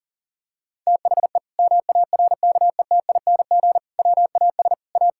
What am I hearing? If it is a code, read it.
Morse code, heard as a keyed tone, about 40 wpm, 700 Hz. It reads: THE MARKETING WAS A
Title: Sentences from Top 1000 Words - Review Part 5 - 40wpm